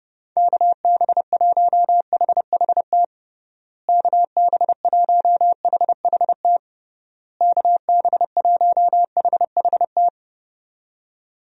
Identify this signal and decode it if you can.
Morse code, heard as a keyed tone, about 30 wpm, 700 Hz. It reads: KB1HHT KB1HHT KB1HHT
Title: USA Call Signs - 6 Characters - Repeated 3x - 30wpm